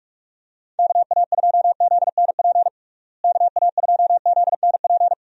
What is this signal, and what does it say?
Morse code, heard as a keyed tone, about 46 wpm, 700 Hz. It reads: KA2ZNP KA2ZNP
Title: USA Call Signs - 6 Characters - Repeated 2x - 45wpm